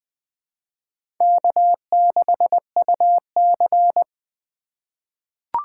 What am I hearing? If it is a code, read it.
Morse code, heard as a keyed tone, about 20 wpm, 700 Hz. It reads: K6UC
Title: USA Call Signs - 4 Characters - 2x Character Spacing - 20wpm